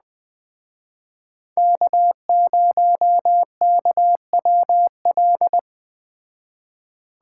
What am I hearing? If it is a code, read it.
Morse code, heard as a keyed tone, about 20 wpm, 700 Hz. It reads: K0KWL